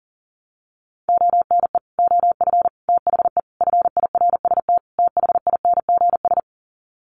Morse code, heard as a keyed tone, about 40 wpm, 700 Hz. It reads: ONE OF THE FIRST THINGS
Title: Sets of 5 Words Encore 40wpm